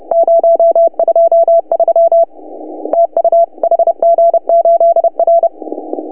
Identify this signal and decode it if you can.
Morse code, heard as a keyed tone, about 29 wpm, 690 Hz. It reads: 023 TU HG8R
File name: CW / HA2OS RHOB 2017 CW